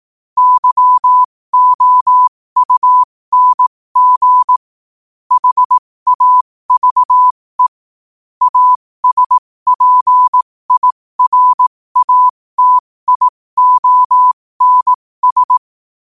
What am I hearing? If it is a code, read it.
Morse code, heard as a keyed tone, about 18 wpm, spaced as for 15 wpm, 1000 Hz. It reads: YOUNG HAVE ASPIRATIONS
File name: qotdfiles / 15wpm